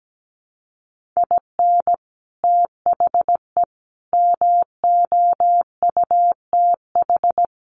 Morse code, heard as a keyed tone, about 17 wpm, 700 Hz. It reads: IN THE MOUTH
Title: Sentences from Top 1500 Words 17wpm